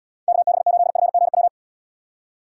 Morse code, heard as a keyed tone, about 79 wpm, 700 Hz. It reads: KB8FGW